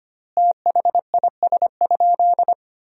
Morse code, heard as a keyed tone, about 25 wpm, 700 Hz. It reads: THIS?
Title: 3x Word-Spacing - Sentences from Top 100 Words - 25wpm